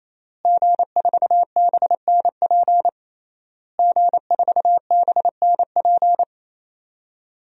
Morse code, heard as a keyed tone, about 28 wpm, 700 Hz. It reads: G4BNP G4BNP
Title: DXCC-100 Call Signs - 5 Characters - Repeated 3x - 28wpm